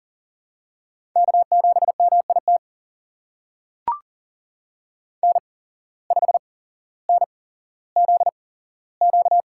Morse code, heard as a keyed tone, about 39 wpm, spaced as for 17 wpm, 700 Hz. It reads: K7MIT N5NZQ
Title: USA Call Signs - 5 Characters - 8x Character Spacing - 40wpm